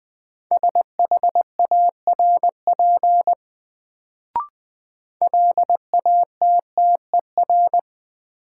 Morse code, heard as a keyed tone, about 20 wpm, 700 Hz. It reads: SHARP LATTER
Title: Sentences from Top 1600 Words - New Word Drill - 20wpm